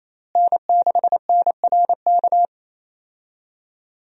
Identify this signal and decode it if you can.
Morse code, heard as a keyed tone, about 28 wpm, 700 Hz. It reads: N6NRK
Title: USA Call Signs - 5 Characters - Repeated 2x - 28wpm